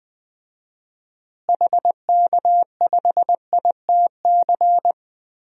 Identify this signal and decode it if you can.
Morse code, heard as a keyed tone, about 20 wpm, 700 Hz. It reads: HK5ITC
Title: DXCC-100 Call Signs - 6 Characters - Repeated 2x - 20wpm